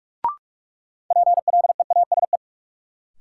Morse code, heard as a keyed tone, about 41 wpm, 700 Hz. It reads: PLEASE
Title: Sentences from Top 1500 Words - New Word Drill - 45wpm